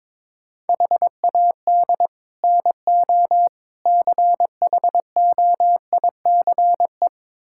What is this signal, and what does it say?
Morse code, heard as a keyed tone, about 21 wpm, 700 Hz. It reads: HAD NO CHOICE
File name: practice / 22wpm / Build A Phrase - 5-Word Phrases - 22wpm